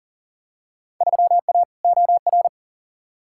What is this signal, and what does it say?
Morse code, heard as a keyed tone, about 40 wpm, 700 Hz. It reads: 3A OR
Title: Field Day Exchanges - 40wpm